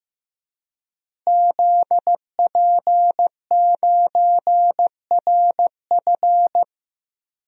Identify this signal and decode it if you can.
Morse code, heard as a keyed tone, about 15 wpm, 700 Hz. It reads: ZP9RF
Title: DXCC-100 Call Signs - 5 Characters - Repeated 3x - 15wpm